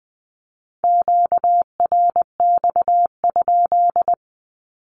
Morse code, heard as a keyed tone, about 20 wpm, 700 Hz. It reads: QRX?
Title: Q-Codes 20wpm